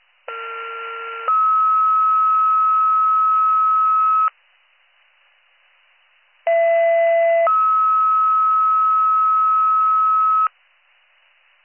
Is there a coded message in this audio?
Yes: two-tone page from 0.3 to 10.5 s, heard as a one-second tone, then a three-second tone, about 1370 then 1290 Hz, about 670 then 1290 Hz.